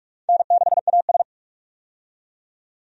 Morse code, heard as a keyed tone, about 48 wpm, 700 Hz. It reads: N6AS
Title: USA Call Signs - 4 Characters - 45wpm